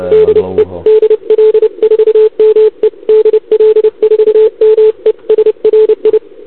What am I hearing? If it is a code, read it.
Morse code, heard as a keyed tone, about 30 wpm, 870 Hz, with noise. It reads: DE DL4ME DL4ME SRI